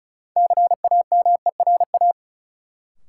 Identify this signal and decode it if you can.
Morse code, heard as a keyed tone, about 34 wpm, 700 Hz. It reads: CAMERA
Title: Sentences from Top 1000 Words - Review Part 5 - New Word Drill - 35wpm